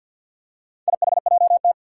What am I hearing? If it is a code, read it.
Morse code, heard as a keyed tone, about 51 wpm, 700 Hz. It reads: IHJT
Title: Four Challenging Letter-Number Combinations 50wpm